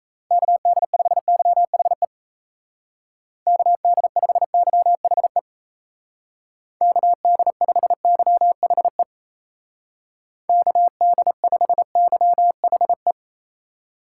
Morse code, heard as a keyed tone, about 34 wpm, 700 Hz. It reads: KD5YHE KD5YHE KD5YHE KD5YHE
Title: Speed Racing - USA Call Signs 28wpm